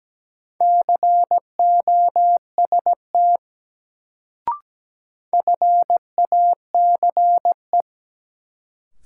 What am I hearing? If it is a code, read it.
Morse code, heard as a keyed tone, about 17 wpm, 700 Hz. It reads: COST FACE
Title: Sets of 3 Words - New Word Drill - 17wpm